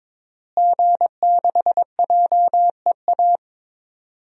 Morse code, heard as a keyed tone, about 22 wpm, 700 Hz. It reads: G6JEA
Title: DXCC-100 Call Signs - 5 Characters - Repeated 3x - 22wpm